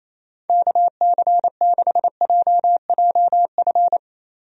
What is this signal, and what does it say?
Morse code, heard as a keyed tone, about 28 wpm, 700 Hz. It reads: KC6JJF
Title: USA Call Signs - 6 Characters - Repeated 2x - 28wpm